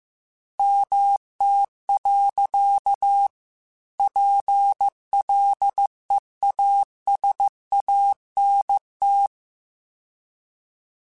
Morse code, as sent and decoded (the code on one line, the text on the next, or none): -- - .-.-.- / .--. .-.. .- ... .- -. -
MT. PLEASANT